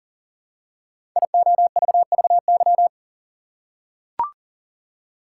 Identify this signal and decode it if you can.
Morse code, heard as a keyed tone, about 39 wpm, 700 Hz. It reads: IOVVY
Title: Code Groups - 5 Letter:Numbers - 40wpm